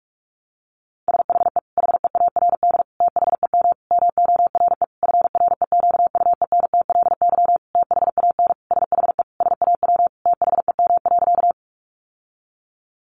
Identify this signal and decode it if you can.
Morse code, heard as a keyed tone, about 44 wpm, 700 Hz. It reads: SHE HEARD THEM MORE FREQUENTLY THAN SHE SAW THEM.
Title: Sentences from Top 1300 Words 45wpm